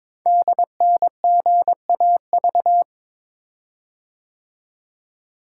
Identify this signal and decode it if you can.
Morse code, heard as a keyed tone, about 22 wpm, 700 Hz. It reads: DNGAV
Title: Code Groups - 5 Letters - 22wpm